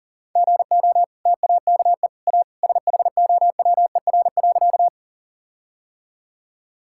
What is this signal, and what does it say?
Morse code, heard as a keyed tone, about 40 wpm, 700 Hz. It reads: GO TAKE A SHOWER.
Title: Sentences from Top 1600 Words 40wpm